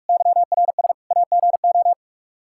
Morse code, heard as a keyed tone, about 41 wpm, 700 Hz. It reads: YRS AGO
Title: Practice QSO - Exchange 3 - More Info - 45wpm